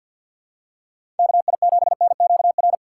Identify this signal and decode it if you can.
Morse code, heard as a keyed tone, about 52 wpm, 700 Hz. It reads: KI7NQR
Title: USA Call Signs - 6 Characters - 3x Character Spacing - 50wpm